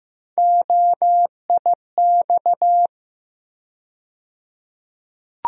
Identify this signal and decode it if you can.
Morse code, heard as a keyed tone, about 15 wpm, 700 Hz. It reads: OIX
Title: USA Call Sign Postfixes - 3 Letters 15wpm